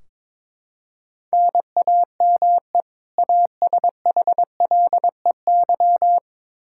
Morse code, heard as a keyed tone, about 22 wpm, 700 Hz. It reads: NAME ASHLEY
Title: Practice QSO - Exchange 1 - Beginning - 22wpm